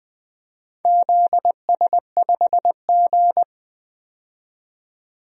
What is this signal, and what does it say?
Morse code, heard as a keyed tone, about 20 wpm, 700 Hz. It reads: ZS5G
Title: DXCC-100 Call Signs - 4 Characters - 20wpm